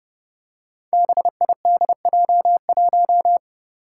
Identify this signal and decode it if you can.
Morse code, heard as a keyed tone, about 29 wpm, 700 Hz. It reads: BIDJ1
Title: Five Challenging Letter-Number Combinations 30wpm